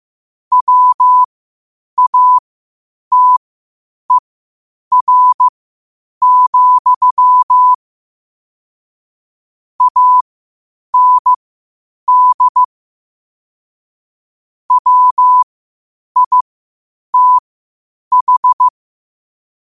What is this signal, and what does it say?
Morse code, heard as a keyed tone, about 15 wpm, spaced as for 8 wpm, 1000 Hz. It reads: WATER, AND WITH
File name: qotdfiles / 8wpm